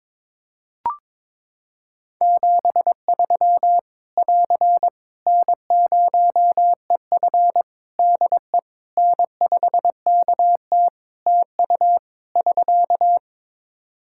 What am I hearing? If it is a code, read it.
Morse code, heard as a keyed tone, about 23 wpm, 700 Hz. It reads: 73 <AR> N0EF DE N5KT TU <SK>
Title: Practice QSO - Exchange 4 - Ending - 22wpm